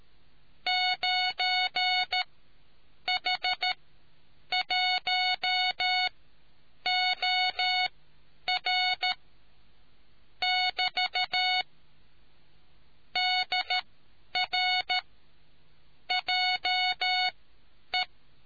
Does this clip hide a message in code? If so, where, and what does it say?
Yes: Morse code from 0.7 to 18.0 s, heard as a keyed tone, about 13 wpm, spaced as for 8 wpm, 740 Hz. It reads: 9H1OR <BT> DR JE